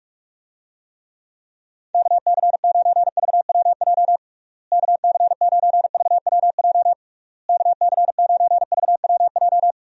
Morse code, heard as a keyed tone, about 42 wpm, 700 Hz. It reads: KC9VWJ KC9VWJ KC9VWJ
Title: USA Call Signs - 6 Characters - Repeated 3x - 45wpm